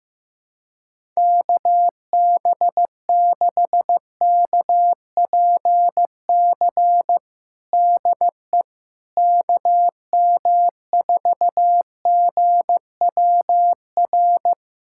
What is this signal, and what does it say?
Morse code, heard as a keyed tone, about 16 wpm, 700 Hz. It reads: KB6KPC DE KM4GWR
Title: USA Callsign DE Callsign - 6 Characters - 15wpm